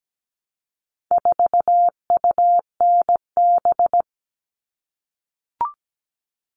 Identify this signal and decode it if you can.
Morse code, heard as a keyed tone, about 17 wpm, 700 Hz. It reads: 4UNB
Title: All Cut Numbers - 4-Digit 17wpm